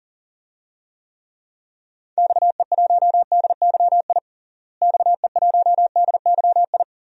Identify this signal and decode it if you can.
Morse code, heard as a keyed tone, about 40 wpm, 700 Hz. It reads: XE1DYI XE1DYI
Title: DXCC-100 Call Signs - 6 Characters - Repeated 3x - 40wpm